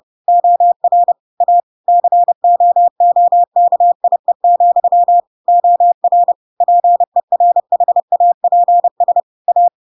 Morse code, heard as a keyed tone, about 30 wpm, 700 Hz. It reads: OR A COOKIE, OR PERHAPS A
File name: Pinocchio / Pinocchio ch5 30 30 ik1hge